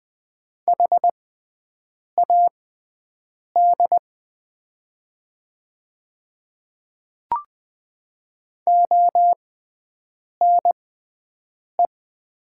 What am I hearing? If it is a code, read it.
Morse code, heard as a keyed tone, about 20 wpm, spaced as for 7 wpm, 700 Hz. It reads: HAD ONE
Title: 6x Character Spacing - Top 100 Words - 20wpm